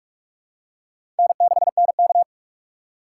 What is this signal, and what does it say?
Morse code, heard as a keyed tone, about 44 wpm, 690 Hz. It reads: N6NK
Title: USA Call Signs - 4 Characters - 2x Character Spacing - 45wpm